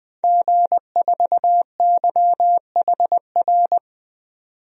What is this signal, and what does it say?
Morse code, heard as a keyed tone, about 20 wpm, 700 Hz. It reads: G4YHR
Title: DXCC-100 Call Signs - 5 Characters - 20wpm